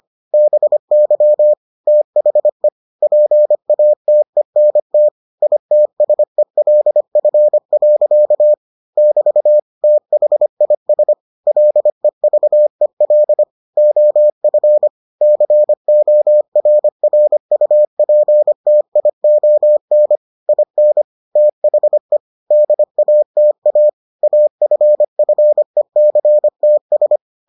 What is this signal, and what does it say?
Morse code, heard as a keyed tone, about 25 wpm, 600 Hz. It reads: BY THE PATENT ITSELF. <BT> THIS LEVEL OF CORRUPTION IN THE DATA AFFECTS